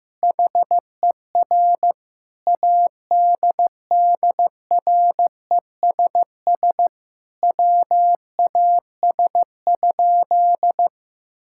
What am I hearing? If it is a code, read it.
Morse code, heard as a keyed tone, about 15 wpm, 700 Hz. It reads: HER ADDRESS WAS?
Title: Sentences from Top 1000 Words - Review Part 6 - 15wpm